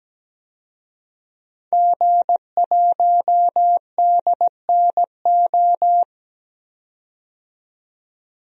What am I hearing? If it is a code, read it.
Morse code, heard as a keyed tone, about 17 wpm, 700 Hz. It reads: G1DNO
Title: DXCC-100 Call Signs - 5 Characters - 17wpm